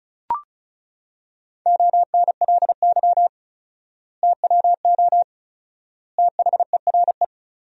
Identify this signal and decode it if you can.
Morse code, heard as a keyed tone, about 36 wpm, 700 Hz. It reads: ONLY TWO THERE
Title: 4x Word-Spacing - Sentences from Top 200 Words - 35wpm